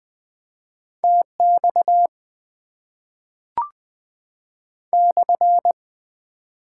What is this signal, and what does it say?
Morse code, heard as a keyed tone, about 20 wpm, 690 Hz. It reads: TX /
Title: Top 90 QSO Elements - 4x Character Spacing - 20wpm